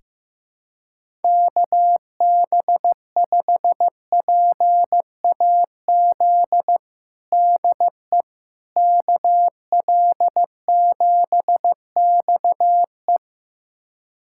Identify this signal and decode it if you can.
Morse code, heard as a keyed tone, about 15 wpm, 700 Hz. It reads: KB5PAZ DE KL7XE
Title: USA Callsign DE Callsign 15wpm